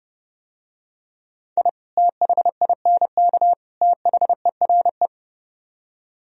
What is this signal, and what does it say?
Morse code, heard as a keyed tone, about 32 wpm, 700 Hz. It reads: I THINK THERE